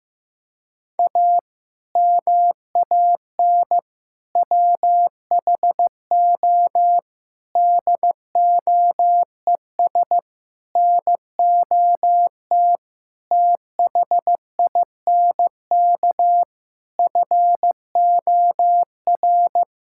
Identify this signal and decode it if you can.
Morse code, heard as a keyed tone, about 15 wpm, 700 Hz. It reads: A MAN WHO DOES NOT THINK FOR
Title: Sentences from Top 1600 Words 15wpm